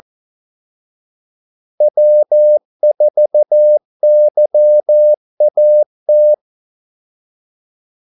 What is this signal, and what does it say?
Morse code, heard as a keyed tone, about 14 wpm, 600 Hz. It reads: W4YAT
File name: Callsigns-14wpm